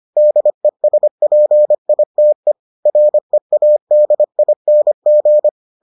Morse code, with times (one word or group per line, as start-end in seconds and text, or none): 0.2-2.5 DESPITE
2.9-5.5 READING